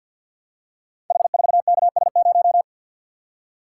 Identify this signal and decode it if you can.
Morse code, heard as a keyed tone, about 46 wpm, 700 Hz. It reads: H4KS0
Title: Code Groups - 5 Letter:Numbers - 50wpm